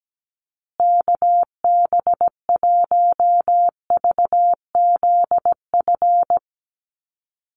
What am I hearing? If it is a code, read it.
Morse code, heard as a keyed tone, about 17 wpm, 700 Hz. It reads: KB1VZF
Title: USA Call Signs Repeated 2x 17wpm